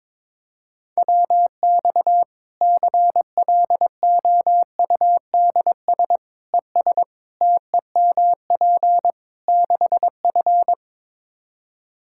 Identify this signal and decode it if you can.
Morse code, heard as a keyed tone, about 22 wpm, 700 Hz. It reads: WX CLOUDS ES TEMP 6F
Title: Practice QSO - Exchange 2 - Info - 22wpm